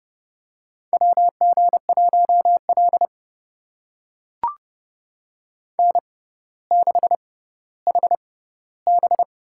Morse code, heard as a keyed tone, about 30 wpm, spaced as for 15 wpm, 700 Hz. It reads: WG1L N6HB
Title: USA Call Signs - 6x Character Spacing - 30wpm